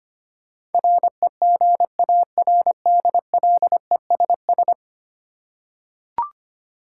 Morse code, heard as a keyed tone, about 25 wpm, 700 Hz. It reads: REGARDLESS
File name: 10 Letter Words - First 3 Letters Given - 25wpm